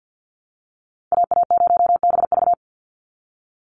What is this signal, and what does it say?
Morse code, heard as a keyed tone, about 52 wpm, 700 Hz. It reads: AA0BV
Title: NA SOTA Activator Callsigns 50wpm